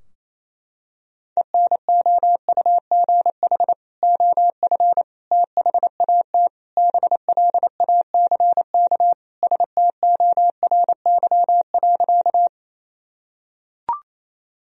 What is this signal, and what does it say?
Morse code, heard as a keyed tone, about 28 wpm, 700 Hz. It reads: ENOUGH OF THAT BLACK STORY.
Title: Sentences from Top 1000 Words - Review Part 2 - 28wpm